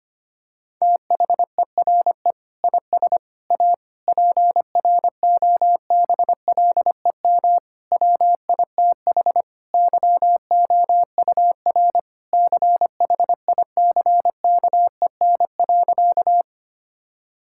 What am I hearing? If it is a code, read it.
Morse code, heard as a keyed tone, about 25 wpm, 700 Hz. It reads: THERE IS A PROBLEM WITH YOUR CHICKEN.
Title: Sentences from Top 1000 Words - Review Part 5 - 25wpm